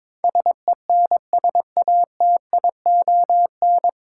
Morse code, heard as a keyed tone, about 22 wpm, 700 Hz. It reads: SENSATION
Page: Morse Code Ninja practice 22wpm